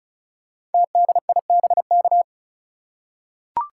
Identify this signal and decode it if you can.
Morse code, heard as a keyed tone, about 36 wpm, 700 Hz. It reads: TDIBK
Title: Code Groups - 5 Letters - 35wpm